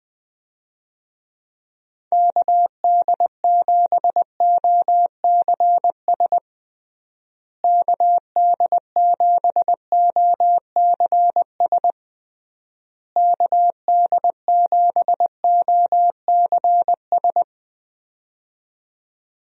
Morse code, heard as a keyed tone, about 21 wpm, 700 Hz. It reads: KD7OCS KD7OCS KD7OCS
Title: USA Call Signs - 6 Characters - Repeated 3x - 20wpm